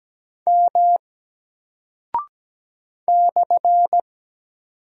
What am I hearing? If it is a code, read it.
Morse code, heard as a keyed tone, about 17 wpm, 700 Hz. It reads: M /
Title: Single Character (Letter:Number:Punctuation:Prosign) Alt 17wpm